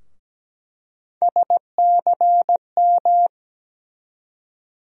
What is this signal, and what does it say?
Morse code, heard as a keyed tone, about 17 wpm, 700 Hz. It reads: SCM
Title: USA Call Sign Postfixes - 3 Letters 17wpm